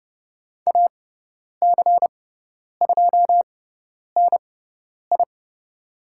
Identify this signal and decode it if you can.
Morse code, heard as a keyed tone, about 30 wpm, spaced as for 10 wpm, 700 Hz. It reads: AC2NI
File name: Lesson 17g - Letter u - Call Signs 30wpm